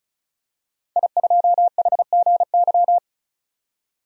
Morse code, heard as a keyed tone, about 35 wpm, 700 Hz. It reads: I2HGY
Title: DXCC-100 Call Signs - 5 Characters - 35wpm